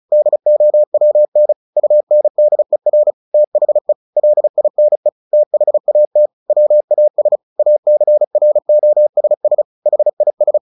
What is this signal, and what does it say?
Morse code, heard as a keyed tone, about 34 wpm, 600 Hz. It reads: DOWN UNDER THE LINE THAT WAS ACROSS HIS
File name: the old man and the sea / the old man and the sea 1 35wpm